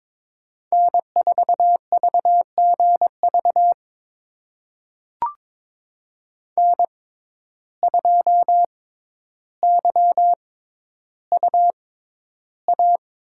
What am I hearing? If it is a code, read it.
Morse code, heard as a keyed tone, about 22 wpm, spaced as for 11 wpm, 700 Hz. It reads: N4VGV N2YUA
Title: USA Call Signs - 5 Characters - 6x Character Spacing - 22wpm